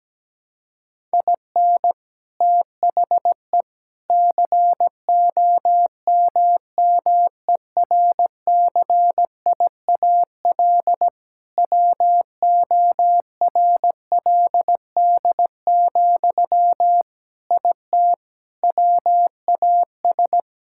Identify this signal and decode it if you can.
Morse code, heard as a keyed tone, about 17 wpm, 700 Hz. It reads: IN THE COMMERCIAL WORLD, IT WAS
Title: Sentences from Top 1000 Words - Review Part 5 - 17wpm